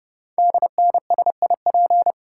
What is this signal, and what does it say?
Morse code, heard as a keyed tone, about 30 wpm, 700 Hz. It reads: DNSIP
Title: Code Groups - 5 Letters - 30wpm